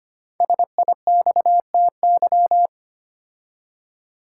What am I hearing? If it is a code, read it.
Morse code, heard as a keyed tone, about 25 wpm, 700 Hz. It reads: SIXTY
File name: 5 Letter Words - First 3 Letters Given - 25wpm